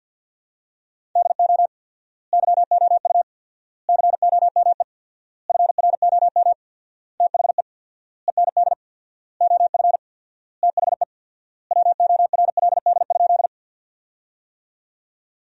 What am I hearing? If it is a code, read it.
Morse code, heard as a keyed tone, about 50 wpm, 700 Hz. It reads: DO YOU COME FROM THE END OF THE WORLD?
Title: 4x Word-Spacing - Sentences from Top 200 Words - 50wpm